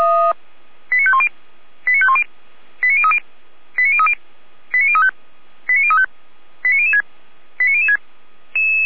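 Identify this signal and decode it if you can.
Five-tone selective call (ZVEI), heard as quick runs of five tones, tones of about 70 ms. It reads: 89510 89510 89030 89030 89035 89035 89006 89006
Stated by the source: ZVEI five-tone selective call received by SDR, Feuerwehr Orlamünde (f85360).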